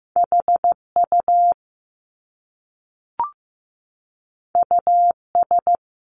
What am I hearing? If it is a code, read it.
Morse code, heard as a keyed tone, about 15 wpm, 700 Hz. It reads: HU US
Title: Top Two Letter Combinations in Words 15wpm